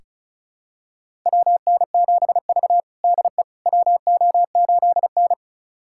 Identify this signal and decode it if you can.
Morse code, heard as a keyed tone, about 34 wpm, 700 Hz. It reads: WN7V DE WO8N
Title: USA Callsign DE Callsign - 4 Characters - 35wpm